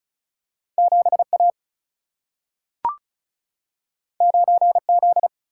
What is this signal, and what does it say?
Morse code, heard as a keyed tone, about 36 wpm, 700 Hz. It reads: 7A 9Z
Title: Single-Letter Single-Number 35wpm